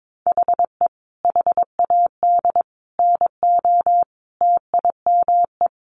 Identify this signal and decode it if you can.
Morse code, heard as a keyed tone, about 22 wpm, 700 Hz. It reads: HE HAD NO TIME
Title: Sentences from Top 700 Words 22wpm